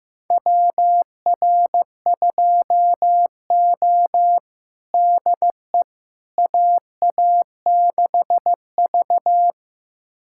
Morse code, heard as a keyed tone, about 16 wpm, 700 Hz. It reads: WR2O DE AA6V
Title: USA Callsign DE Callsign - 4 Characters - 15wpm